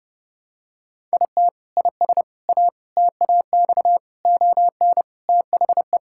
Morse code, heard as a keyed tone, about 30 wpm, 700 Hz. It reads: IT IS A TAX ON THE
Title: Sentences from Top 1000 Words - Review Part 5 - 30wpm